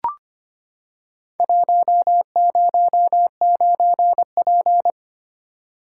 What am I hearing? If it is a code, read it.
Morse code, heard as a keyed tone, about 25 wpm, 700 Hz. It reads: 109P
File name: practice / 25wpm / Four Letter-Number Combinations - 25wpm